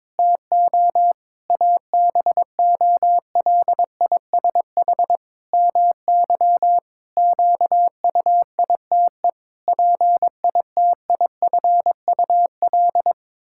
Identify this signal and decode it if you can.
Morse code, heard as a keyed tone, about 22 wpm, 700 Hz. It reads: TO ABOLISH MY QUITE PITIFUL
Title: Pangram Challenge v2 - 22wpm